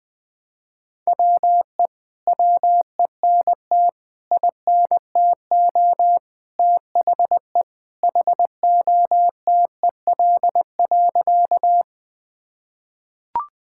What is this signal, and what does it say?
Morse code, heard as a keyed tone, about 20 wpm, 700 Hz. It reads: WE WENT INTO THE HOTEL.
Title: Sentences from Top 1600 Words 20wpm